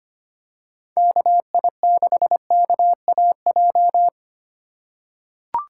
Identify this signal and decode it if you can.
Morse code, heard as a keyed tone, about 25 wpm, 700 Hz. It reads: KI6KAJ
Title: USA Call Signs - 6 Characters - 4x Character Spacing - 25wpm